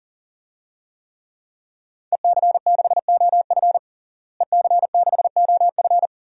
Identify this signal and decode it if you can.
Morse code, heard as a keyed tone, about 40 wpm, 700 Hz. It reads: EC6OF EC6OF
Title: DXCC-100 Call Signs - 5 Characters - Repeated 2x - 40wpm